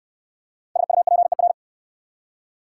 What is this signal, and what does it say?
Morse code, heard as a keyed tone, about 65 wpm, 700 Hz. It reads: SUPER